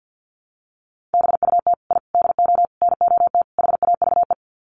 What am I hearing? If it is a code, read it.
Morse code, heard as a keyed tone, about 50 wpm, 700 Hz. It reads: BUT I DO NOT HAVE
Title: Sets of 5 Words Encore 50wpm